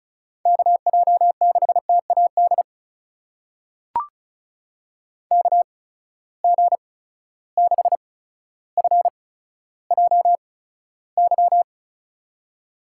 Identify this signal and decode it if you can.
Morse code, heard as a keyed tone, about 36 wpm, spaced as for 15 wpm, 700 Hz. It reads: KJ6TAD KG6FJY
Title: USA Call Signs - 6 Characters - 8x Character Spacing - 35wpm